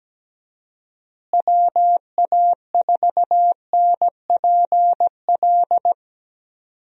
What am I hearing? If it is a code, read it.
Morse code, heard as a keyed tone, about 17 wpm, 700 Hz. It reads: WA4NPL
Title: USA Call Signs - 6 Characters - Repeated 3x - 17wpm